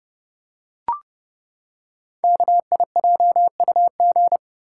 Morse code, heard as a keyed tone, about 30 wpm, 700 Hz. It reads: KIJUG
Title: Code Groups - 5 Letters - 30wpm